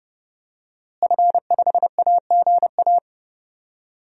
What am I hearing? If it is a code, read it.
Morse code, heard as a keyed tone, about 30 wpm, 700 Hz. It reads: F5AGA
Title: DXCC-100 Call Signs - 5 Characters - Repeated 2x - 30wpm